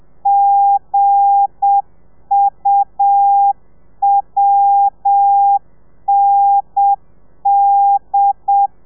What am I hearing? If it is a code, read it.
Morse code, heard as a keyed tone, about 7 wpm, 800 Hz. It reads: GUWND
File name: telegrafia / i2tfj / Lez8 30